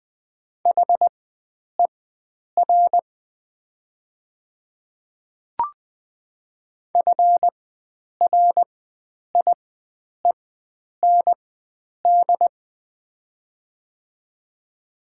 Morse code, heard as a keyed tone, about 20 wpm, spaced as for 9 wpm, 700 Hz. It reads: HER FRIEND
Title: 4x Character Spacing - Top 300 Words - 20wpm